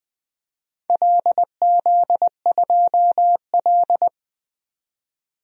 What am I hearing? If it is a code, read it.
Morse code, heard as a keyed tone, about 21 wpm, 700 Hz. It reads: LZ2L
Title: DXCC-100 Call Signs - 4 Characters - 20wpm